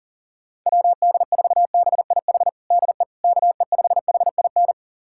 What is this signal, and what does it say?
Morse code, heard as a keyed tone, about 40 wpm, 700 Hz. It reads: WD4BIH DE KE5HIN
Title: USA Callsign DE Callsign 40wpm